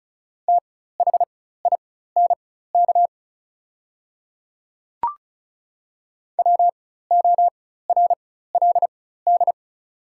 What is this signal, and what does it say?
Morse code, heard as a keyed tone, about 36 wpm, spaced as for 16 wpm, 700 Hz. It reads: THINK WORLD
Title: 4x Character Spacing - Top 200 Words - 35wpm